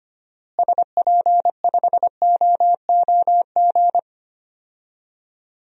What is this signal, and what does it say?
Morse code, heard as a keyed tone, about 25 wpm, 700 Hz. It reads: SP5OOG
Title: DXCC-100 Call Signs - 6 Characters - Repeated 3x - 25wpm